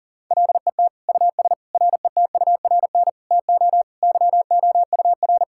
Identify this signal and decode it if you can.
Morse code, heard as a keyed tone, about 37 wpm, 700 Hz. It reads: LET US RETURN TO YOUR
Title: Sentences from Top 1000 Words - Review Part 6 - 40wpm